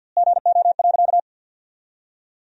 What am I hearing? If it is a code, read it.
Morse code, heard as a keyed tone, about 50 wpm, 700 Hz. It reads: GO.
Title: 2x Word-Spacing - Sentences from Top 400 Words - 50wpm